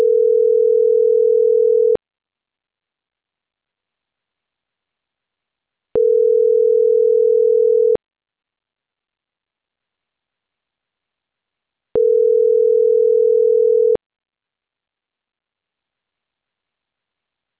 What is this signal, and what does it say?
Ringback tone (North America), 440 + 480 Hz.